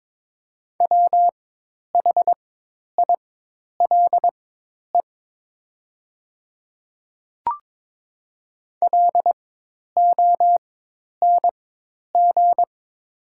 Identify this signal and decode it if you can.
Morse code, heard as a keyed tone, about 22 wpm, spaced as for 10 wpm, 700 Hz. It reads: WHILE LONG
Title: 4x Character Spacing - Top 300 Words - 22wpm